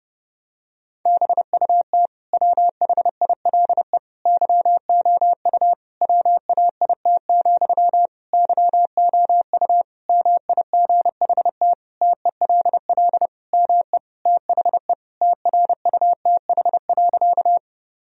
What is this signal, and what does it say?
Morse code, heard as a keyed tone, about 30 wpm, 700 Hz. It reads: BUT WHILE YOU WAIT, YOU MIGHT TELL ME THE TRUTH.
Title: Sentences from Top 1000 Words - Review Part 1 - 30wpm